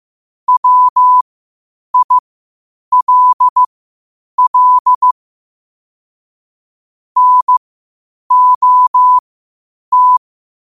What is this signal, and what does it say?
Morse code, heard as a keyed tone, about 15 wpm, spaced as for 8 wpm, 1000 Hz. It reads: WILL NOT
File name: qotdfiles / 8wpm